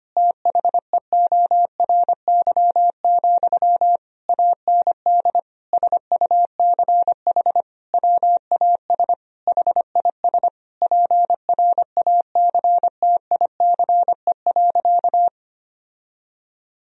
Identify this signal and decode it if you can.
Morse code, heard as a keyed tone, about 25 wpm, 700 Hz. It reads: THEORY, AND SUCH WAS HIS PRACTICE.
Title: Sentences from Top 1000 Words - Review Part 4 - 25wpm